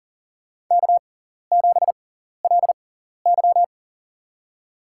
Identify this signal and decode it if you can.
Morse code, heard as a keyed tone, about 40 wpm, spaced as for 14 wpm, 700 Hz. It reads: K7LY
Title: USA Call Signs - 4 Characters - 6x Character Spacing - 40wpm